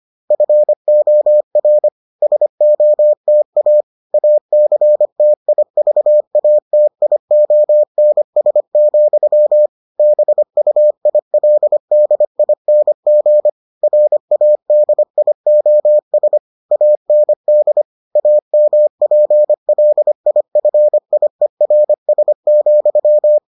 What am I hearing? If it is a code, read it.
Morse code, heard as a keyed tone, about 25 wpm, 600 Hz. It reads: FOR SOTA ACTIVATIONS, BUILDING RADIOS AND AMPLIFIERS,